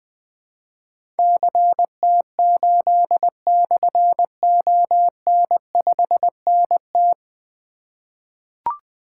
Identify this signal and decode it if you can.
Morse code, heard as a keyed tone, about 20 wpm, 700 Hz. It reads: CT8/ON5NT
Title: Difficult Call Signs 20wpm